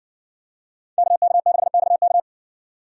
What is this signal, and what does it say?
Morse code, heard as a keyed tone, about 58 wpm, 700 Hz. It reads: KK6XK